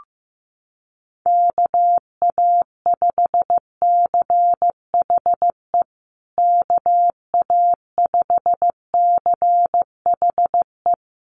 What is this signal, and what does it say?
Morse code, heard as a keyed tone, about 15 wpm, 700 Hz. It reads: KA5CHE KA5CHE
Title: USA Call Signs Repeated 3x 15wpm